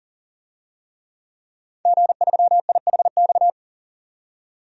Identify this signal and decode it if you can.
Morse code, heard as a keyed tone, about 39 wpm, 700 Hz. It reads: G3IHX
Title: DXCC-100 Call Signs - 5 Characters - 40wpm